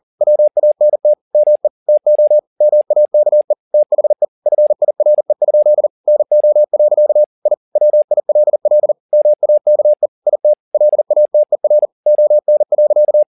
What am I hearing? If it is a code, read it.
Morse code, heard as a keyed tone, about 40 wpm, 600 Hz. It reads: WANT ME TO MAKE THE FIRE? NO. I WILL MAKE IT LATER ON.